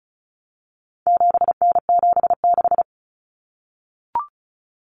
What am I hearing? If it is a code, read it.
Morse code, heard as a keyed tone, about 34 wpm, 700 Hz. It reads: 7N76
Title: Common Cut Numbers - 4-Digit 35wpm